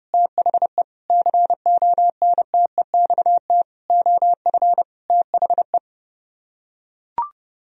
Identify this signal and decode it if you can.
Morse code, heard as a keyed tone, about 30 wpm, 700 Hz. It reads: THE CONTEXT OF THE